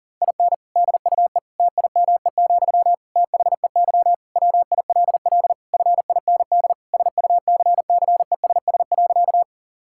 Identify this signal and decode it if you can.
Morse code, heard as a keyed tone, about 40 wpm, 700 Hz. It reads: IN DUE TIME, THEY WILL FIND SUCCESS.
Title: Sentences from Top 1000 Words - Review Part 6 - 40wpm